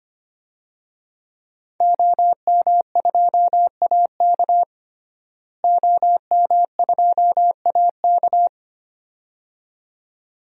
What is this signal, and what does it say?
Morse code, heard as a keyed tone, about 25 wpm, 700 Hz. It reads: OM2AK OM2AK
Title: DXCC-100 Call Signs - 5 Characters - Repeated 2x - 25wpm